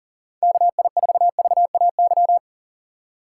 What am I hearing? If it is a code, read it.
Morse code, heard as a keyed tone, about 40 wpm, 700 Hz. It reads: KI4VAY
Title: USA Call Signs - 6x Character Spacing - 40wpm